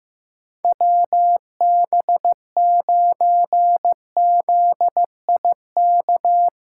Morse code, heard as a keyed tone, about 15 wpm, 700 Hz. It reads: WB9ZIK
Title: USA Callsign DE Callsign - 6 Characters - 15wpm